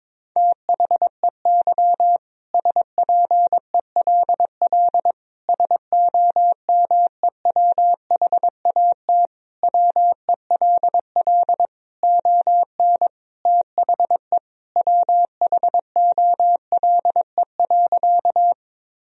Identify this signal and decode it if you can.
Morse code, heard as a keyed tone, about 22 wpm, 700 Hz. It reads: THEY SPELL SOMEWHAT WELL ON THE WHOLE.
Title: Sentences from Top 1000 Words - Review Part 1 - 22wpm